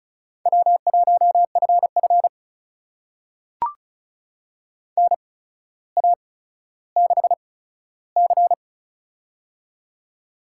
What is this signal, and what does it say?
Morse code, heard as a keyed tone, about 36 wpm, spaced as for 15 wpm, 700 Hz. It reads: W1FF NA6C
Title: USA Call Signs - 4 Characters - 8x Character Spacing - 35wpm